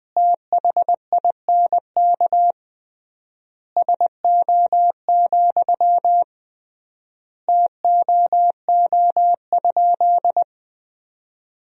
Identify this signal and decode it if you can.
Morse code, heard as a keyed tone, about 20 wpm, 700 Hz. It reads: THINK SO, TOO?
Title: 3x Word-Spacing - Sentences from Top 200 Words - 20wpm